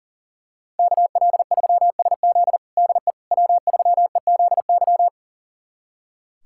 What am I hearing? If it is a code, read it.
Morse code, heard as a keyed tone, about 40 wpm, 700 Hz. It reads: KL3SZ DE W3EZY
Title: USA Callsign DE Callsign - 5 Characters - 40wpm